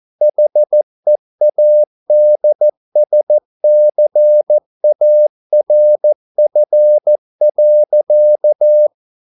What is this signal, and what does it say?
Morse code, heard as a keyed tone, about 14 wpm, 600 Hz. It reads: HEADSCARF.